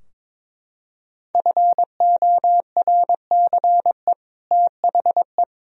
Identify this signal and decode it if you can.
Morse code, heard as a keyed tone, about 22 wpm, 700 Hz. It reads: FORCE THE
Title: Sentences from Top 1000 Words - Review Part 3 - 22wpm